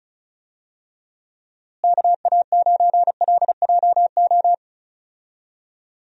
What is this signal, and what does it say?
Morse code, heard as a keyed tone, about 36 wpm, 700 Hz. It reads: KA9LJO
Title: USA Call Signs - 6 Characters - Repeated 2x - 35wpm